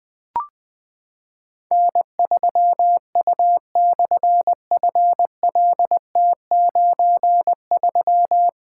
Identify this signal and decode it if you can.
Morse code, heard as a keyed tone, about 20 wpm, 700 Hz. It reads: N3U/FLT93
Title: Difficult Call Signs 20wpm